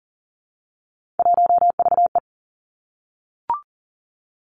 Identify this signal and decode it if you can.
Morse code, heard as a keyed tone, about 42 wpm, 700 Hz. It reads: 1VE
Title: Three Challenging Letter-Number Combinations 40wpm